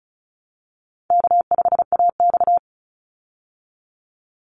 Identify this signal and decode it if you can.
Morse code, heard as a keyed tone, about 34 wpm, 700 Hz. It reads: K5AX
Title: Easy Call Signs Repeated 3x - 35wpm